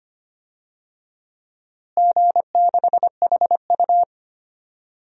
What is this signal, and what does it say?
Morse code, heard as a keyed tone, about 25 wpm, 700 Hz. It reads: G6HU
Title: DXCC-100 Call Signs - 4 Characters - Repeated 2x - 25wpm